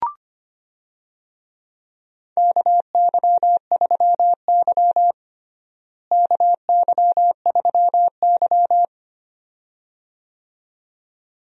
Morse code, heard as a keyed tone, about 25 wpm, 700 Hz. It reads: KY3Y KY3Y
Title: USA Call Signs - 4 Characters - Repeated 2x - 25wpm